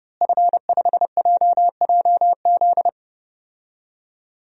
Morse code, heard as a keyed tone, about 30 wpm, 700 Hz. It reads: F5JJZ
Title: DXCC-100 Call Signs - 5 Characters - Repeated 3x - 30wpm